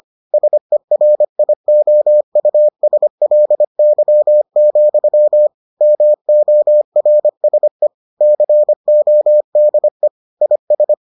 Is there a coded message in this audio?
Yes: Morse code from 0.3 to 10.9 s, heard as a keyed tone, about 25 wpm, 600 Hz. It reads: SERIOUSLY, MORSE CODE IS